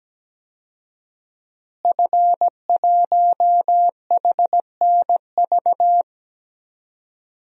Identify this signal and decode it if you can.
Morse code, heard as a keyed tone, about 17 wpm, 700 Hz. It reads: F1HNV
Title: DXCC-100 Call Signs - 5 Characters - Repeated 2x - 17wpm